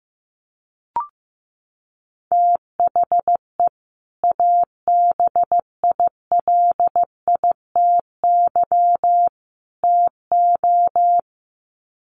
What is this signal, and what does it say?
Morse code, heard as a keyed tone, about 15 wpm, 700 Hz. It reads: THE ABILITY TO
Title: Sets of 3 Words 15wpm